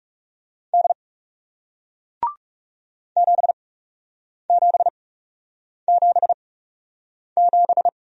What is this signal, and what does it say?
Morse code, heard as a keyed tone, about 40 wpm, spaced as for 8 wpm, 700 Hz. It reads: D 7777